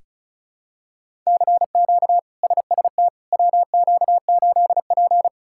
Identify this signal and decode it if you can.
Morse code, heard as a keyed tone, about 34 wpm, 700 Hz. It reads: CQ SST WQ8P
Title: K1USN SST QSO Practice - 35wpm